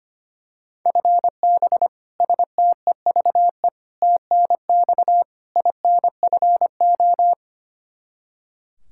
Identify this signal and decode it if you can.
Morse code, heard as a keyed tone, about 25 wpm, 700 Hz. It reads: FB STEVE TNX INFO
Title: Practice QSO - Exchange 3 - More Info - 25wpm